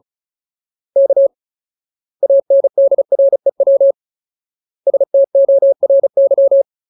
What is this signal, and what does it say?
Morse code, heard as a keyed tone, about 34 wpm, 550 Hz. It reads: K ANDREW STORY
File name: Ultimate Word List / 35WPM Ultimate